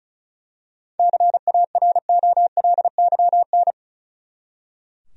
Morse code, heard as a keyed tone, about 34 wpm, 700 Hz. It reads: CAROLYN
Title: USA Common Names - 35wpm